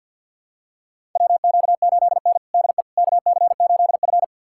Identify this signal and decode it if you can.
Morse code, heard as a keyed tone, about 52 wpm, 700 Hz. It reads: WQ8N DE KC8F
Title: USA Callsign DE Callsign - 4 Characters - 50wpm